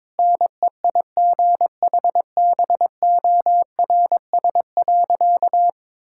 Morse code, heard as a keyed tone, about 22 wpm, 700 Hz. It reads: NEIGHBORS.
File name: practice / 22wpm / Sentences with 3-Word Phrases - 22wpm